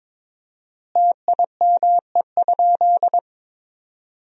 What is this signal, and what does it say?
Morse code, heard as a keyed tone, about 22 wpm, 700 Hz. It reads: TIME?
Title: 4x Word-Spacing - Sentences from Top 100 Words - 22wpm